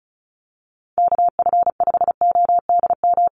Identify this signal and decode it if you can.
Morse code, heard as a keyed tone, about 34 wpm, 700 Hz. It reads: KF5ODM
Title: USA Call Signs Repeated 3x 35wpm